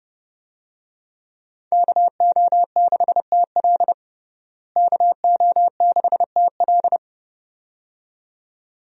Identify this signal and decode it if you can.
Morse code, heard as a keyed tone, about 32 wpm, 700 Hz. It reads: KO6TL KO6TL
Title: USA Call Signs - 5 Characters - Repeated 2x - 30wpm